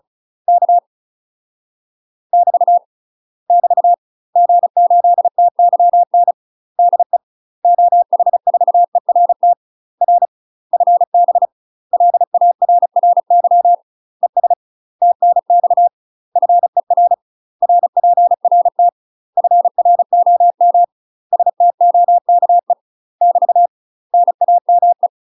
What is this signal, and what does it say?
Morse code, heard as a keyed tone, about 34 wpm, 700 Hz. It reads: K <BT> <BT> G8TYN DE OH4ERT R FB LARRY ES TNX FER RPRT FROM STOKE <BT> NAME
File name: UK MASTER QSOs TRAINING / 35WPM / UK MASTER QSOs 0029